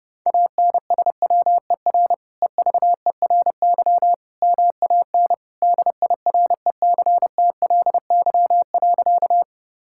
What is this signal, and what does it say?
Morse code, heard as a keyed tone, about 32 wpm, 700 Hz. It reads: ANSWER EVERY MAN DIRECTLY.
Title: Sentences from Top 1000 Words - Review Part 5 - 30wpm